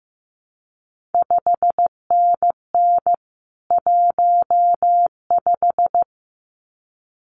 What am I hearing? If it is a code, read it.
Morse code, heard as a keyed tone, about 15 wpm, 700 Hz. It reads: 5NN 15
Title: CQ WW DX Contest Exchange Encore - 15wpm